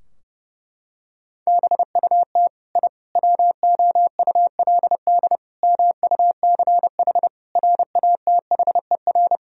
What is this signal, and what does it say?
Morse code, heard as a keyed tone, about 30 wpm, 700 Hz. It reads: BUT I WOULD MUCH RATHER
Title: Sentences from Top 1000 Words - Review Part 4 - 30wpm